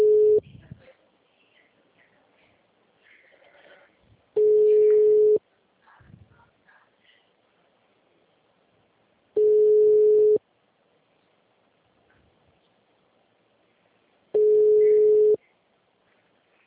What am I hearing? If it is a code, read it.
Ringback tone (Europe, 425 Hz).
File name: out- -2009- - - .53314